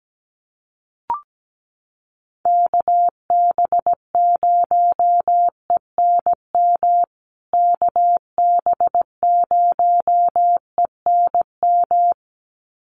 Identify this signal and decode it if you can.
Morse code, heard as a keyed tone, about 17 wpm, 700 Hz. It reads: KB0ENM KB0ENM
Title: USA Call Signs Repeated 2x 17wpm